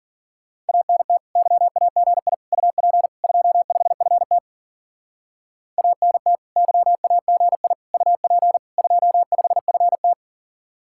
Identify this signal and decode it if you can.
Morse code, heard as a keyed tone, about 52 wpm, 700 Hz. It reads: ANT YAGI UP 25FT ANT YAGI UP 25FT